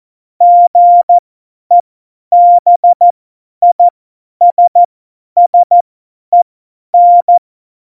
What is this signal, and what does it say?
Morse code, heard as a keyed tone, about 14 wpm, spaced as for 10 wpm, 700 Hz. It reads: GEBISSEN
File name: Morsekurs / Witze / Witz 17 April 2018, 50 F70